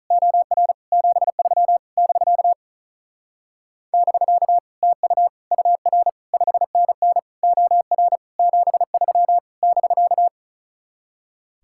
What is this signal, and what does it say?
Morse code, heard as a keyed tone, about 45 wpm, 700 Hz. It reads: OR 73 <BK> <BK> TU UR 5NN OR 73 <BK>